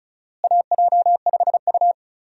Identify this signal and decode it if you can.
Morse code, heard as a keyed tone, about 36 wpm, 700 Hz. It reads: AJ5U
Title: USA Call Signs - 4 Characters - Repeated 3x - 35wpm